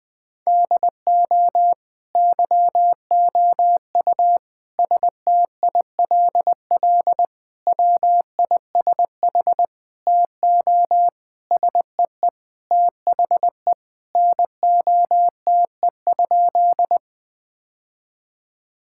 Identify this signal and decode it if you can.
Morse code, heard as a keyed tone, about 20 wpm, 700 Hz. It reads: DO YOU STILL WISH TO SEE THE NOTE?
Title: Sentences from Top 1600 Words 20wpm